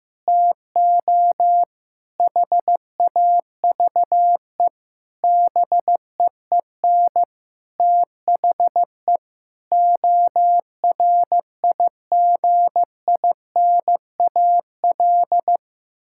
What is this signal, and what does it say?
Morse code, heard as a keyed tone, about 15 wpm, 700 Hz. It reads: TO HAVE BEEN THE ORIGINAL
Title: Sentences from Top 1000 Words - Review Part 6 - 15wpm